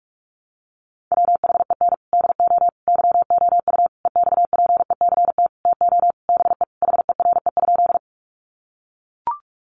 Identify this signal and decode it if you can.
Morse code, heard as a keyed tone, about 44 wpm, 700 Hz. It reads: WHEN DO YOU EXPECT TO BE HERE?
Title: Sentences from Top 1100 Words 45wpm